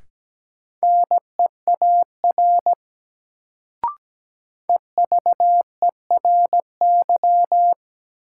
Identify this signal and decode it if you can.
Morse code, heard as a keyed tone, about 17 wpm, 700 Hz. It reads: NEAR EVERY
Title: Sentences from Top 200 Words - New Word Drill - 17wpm